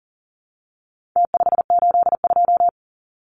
Morse code, heard as a keyed tone, about 40 wpm, 700 Hz. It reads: T582
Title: Numbers - Contest Style - 4-Digit 40wpm